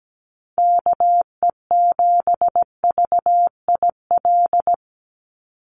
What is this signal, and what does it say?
Morse code, heard as a keyed tone, about 17 wpm, 700 Hz. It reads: KE7VIL
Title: USA Call Signs 17wpm